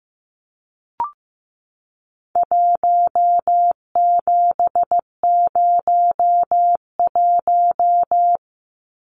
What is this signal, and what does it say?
Morse code, heard as a keyed tone, about 15 wpm, 700 Hz. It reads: 1701
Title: Numbers - 4-Digit 15wpm